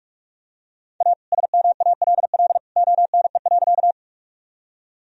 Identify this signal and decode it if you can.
Morse code, heard as a keyed tone, about 41 wpm, 700 Hz. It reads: A SMALL ONE.
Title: Sentences from Top 1000 Words - Review Part 1 - 45wpm